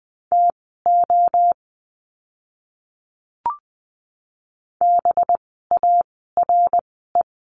Morse code, heard as a keyed tone, about 20 wpm, spaced as for 15 wpm, 700 Hz. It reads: TO BARE
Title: 185 Lesson 30f - Letter b - Words 20wpm